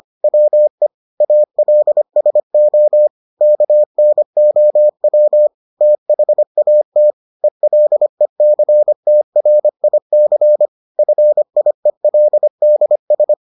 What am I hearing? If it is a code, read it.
Morse code, heard as a keyed tone, about 25 wpm, 600 Hz. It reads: WE ALSO KNOW THAT ELECTRIC FIELDS